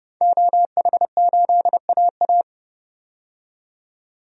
Morse code, heard as a keyed tone, about 30 wpm, 700 Hz. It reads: OH8AA
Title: DXCC-100 Call Signs - 5 Characters - 30wpm